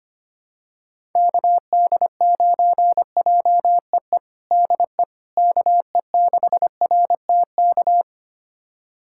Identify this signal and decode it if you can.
Morse code, heard as a keyed tone, about 25 wpm, 700 Hz. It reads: KD9JEE DE KE6RTK
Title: USA Callsign DE Callsign - 6 Characters - 25wpm